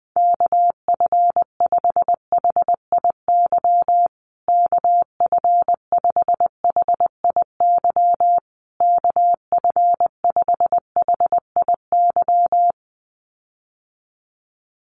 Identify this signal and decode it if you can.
Morse code, heard as a keyed tone, about 20 wpm, 700 Hz. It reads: KF5HIY KF5HIY KF5HIY
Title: USA Call Signs Repeated 3x 20wpm